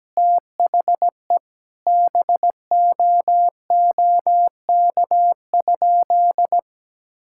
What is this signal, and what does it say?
Morse code, heard as a keyed tone, about 17 wpm, 700 Hz. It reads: THE BOOK?
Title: Sentences from Top 1000 Words - Review Part 5 - 17wpm